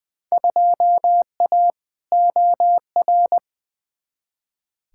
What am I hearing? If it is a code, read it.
Morse code, heard as a keyed tone, about 20 wpm, 700 Hz. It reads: 2A OR
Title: Field Day Exchanges Encore - 20wpm